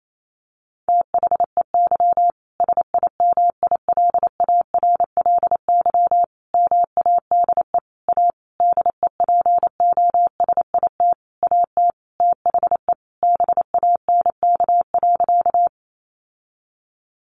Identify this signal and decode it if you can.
Morse code, heard as a keyed tone, about 28 wpm, 700 Hz. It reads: THEY SIMILARLY MADE A DEPOSIT AT THE BANK.
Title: Sentences from Top 1400 Words 28wpm